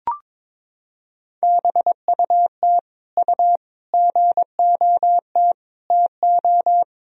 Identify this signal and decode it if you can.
Morse code, heard as a keyed tone, about 22 wpm, 700 Hz. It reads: BUT U GOT TO
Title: Sentences from Top 300 Words and Abbreviations 22wpm